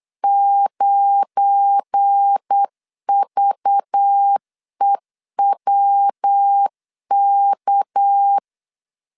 Morse code, heard as a keyed tone, about 8 wpm, 800 Hz. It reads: 9VEWK